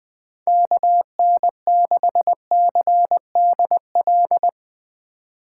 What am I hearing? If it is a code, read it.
Morse code, heard as a keyed tone, about 20 wpm, 700 Hz. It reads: KN6CDL
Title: Speed Racing - USA 2x3 Call Signs 17wpm